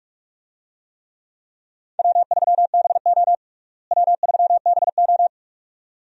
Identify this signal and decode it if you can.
Morse code, heard as a keyed tone, about 46 wpm, 700 Hz. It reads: W3BO W3BO
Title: USA Call Signs - 4 Characters - Repeated 2x - 45wpm